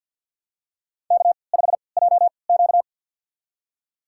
Morse code, heard as a keyed tone, about 47 wpm, spaced as for 28 wpm, 700 Hz. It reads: K5JQ
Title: USA Call Signs - 4 Characters - 3x Character Spacing - 50wpm